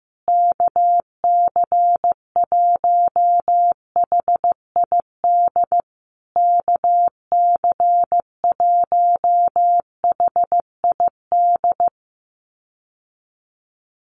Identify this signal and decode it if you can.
Morse code, heard as a keyed tone, about 15 wpm, 700 Hz. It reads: KC1HID KC1HID
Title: USA Call Signs Repeated 3x 15wpm